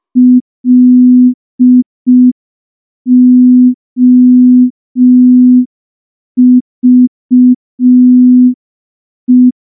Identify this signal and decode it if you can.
Morse code, heard as a keyed tone, about 5 wpm, 250 Hz. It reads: LOVE
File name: morse / morse-training-05WPM